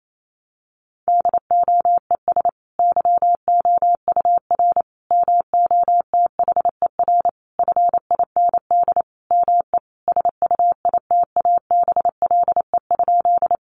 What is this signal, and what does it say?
Morse code, heard as a keyed tone, about 28 wpm, 700 Hz. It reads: DOES YOUR MOTHER FIND ME SUITABLE?
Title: Sentences from Top 1400 Words 28wpm